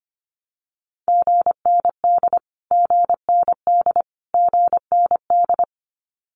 Morse code, heard as a keyed tone, about 25 wpm, 700 Hz. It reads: GND GND GND
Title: Abbreviations Repeated 3x - 25wpm